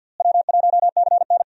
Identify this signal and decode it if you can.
Morse code, heard as a keyed tone, about 51 wpm, 700 Hz. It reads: W1CN
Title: USA Call Signs - 4 Characters - Repeated 3x - 50wpm